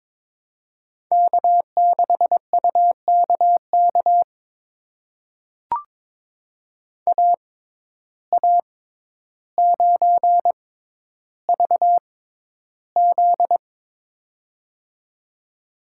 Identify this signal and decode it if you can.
Morse code, heard as a keyed tone, about 22 wpm, spaced as for 11 wpm, 700 Hz. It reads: K6UKK AA9VZ
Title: USA Call Signs - 5 Characters - 6x Character Spacing - 22wpm